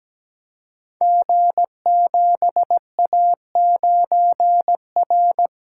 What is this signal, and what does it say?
Morse code, heard as a keyed tone, about 17 wpm, 700 Hz. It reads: G7A9R
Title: Code Groups - 5 Letter:Numbers - 17wpm